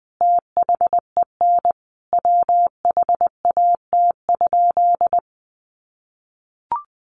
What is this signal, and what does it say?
Morse code, heard as a keyed tone, about 19 wpm, 700 Hz. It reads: THEN WHAT?
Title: Sentences from Top 1300 Words 20wpm